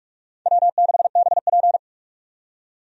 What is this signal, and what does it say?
Morse code, heard as a keyed tone, about 44 wpm, 700 Hz. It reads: W6BP
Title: USA Call Signs - 4 Characters - 45wpm